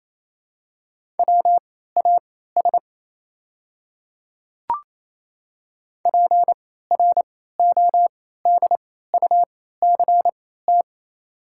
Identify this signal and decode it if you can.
Morse code, heard as a keyed tone, about 27 wpm, spaced as for 16 wpm, 700 Hz. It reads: WAS PRODUCT
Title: 3x Character Spacing - Top 500 Words - 28wpm